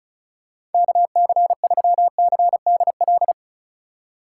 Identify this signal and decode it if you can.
Morse code, heard as a keyed tone, about 36 wpm, 700 Hz. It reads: KC3CDL
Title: USA Call Signs - 6 Characters - 3x Character Spacing - 35wpm